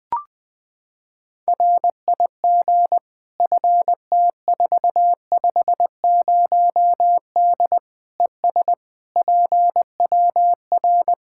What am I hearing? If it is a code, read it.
Morse code, heard as a keyed tone, about 20 wpm, 700 Hz. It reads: RIG FT450D ES PWR
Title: Practice QSO - Exchange 2 - Info - 20wpm